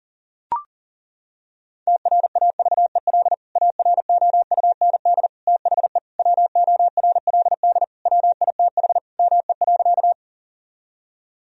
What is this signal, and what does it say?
Morse code, heard as a keyed tone, about 40 wpm, 700 Hz. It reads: TRAVEL AROUND THE WORLD WITH ME.
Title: Sentences from Top 1000 Words - Review Part 3 - 40wpm